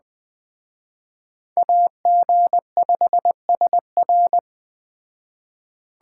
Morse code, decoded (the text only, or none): AG5SR